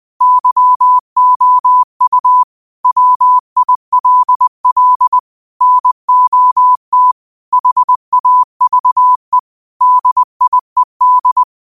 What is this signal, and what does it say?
Morse code, heard as a keyed tone, about 20 wpm, 1000 Hz. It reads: YOU WILL NOT HAVE DIED
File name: qotdfiles / 20wpm